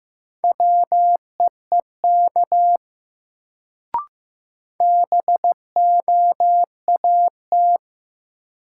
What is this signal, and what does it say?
Morse code, heard as a keyed tone, about 15 wpm, 700 Hz. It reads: WEEK BOAT
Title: Sentences from Top 500 Words - New Word Drill - 15wpm